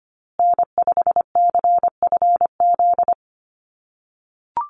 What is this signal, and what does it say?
Morse code, heard as a keyed tone, about 25 wpm, 700 Hz. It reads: N5CFZ
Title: USA Call Signs Repeated 2x 25wpm